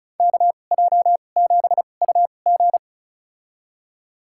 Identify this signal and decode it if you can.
Morse code, heard as a keyed tone, about 36 wpm, spaced as for 25 wpm, 700 Hz. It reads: KJ7UG
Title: USA Call Signs - 5 Characters - 2x Character Spacing - 35wpm